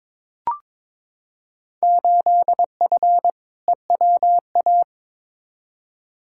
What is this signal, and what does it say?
Morse code, heard as a keyed tone, about 22 wpm, 700 Hz. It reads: 8F EWA
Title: Field Day Exchanges Encore - 22wpm